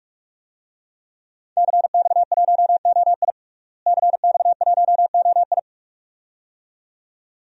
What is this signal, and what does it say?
Morse code, heard as a keyed tone, about 46 wpm, 700 Hz. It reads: CX1OI CX1OI
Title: DXCC-100 Call Signs - 5 Characters - Repeated 2x - 45wpm